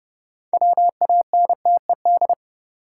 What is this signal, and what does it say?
Morse code, heard as a keyed tone, about 30 wpm, 700 Hz. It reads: WANTED
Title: Sets of 2 Words - New Word Drill - 30wpm